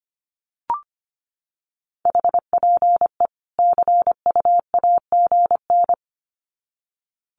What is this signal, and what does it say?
Morse code, heard as a keyed tone, about 25 wpm, 700 Hz. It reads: HPE CUAGN
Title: Common QSO Phrases 25wpm